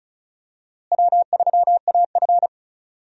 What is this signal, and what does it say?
Morse code, heard as a keyed tone, about 34 wpm, 700 Hz. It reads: W3AF
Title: USA Call Signs - 4 Characters - Repeated 2x - 35wpm